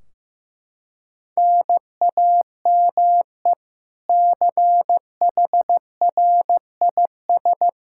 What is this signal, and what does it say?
Morse code, heard as a keyed tone, about 15 wpm, 700 Hz. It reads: NAME CHRIS
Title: Practice QSO - Exchange 1 - Beginning - 15wpm